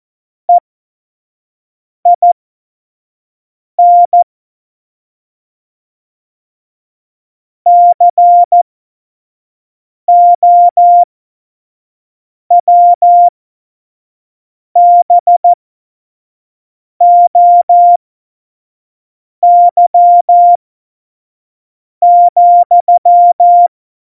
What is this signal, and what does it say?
Morse code, heard as a keyed tone, about 14 wpm, spaced as for 5 wpm, 700 Hz. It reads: EIN COWBOY,
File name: Morsekurs / Witze / Witz 09. April 2019, 25 F70